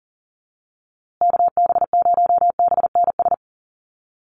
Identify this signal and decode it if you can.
Morse code, heard as a keyed tone, about 40 wpm, 700 Hz. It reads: KB0BNS
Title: USA Call Signs Repeated 2x 40wpm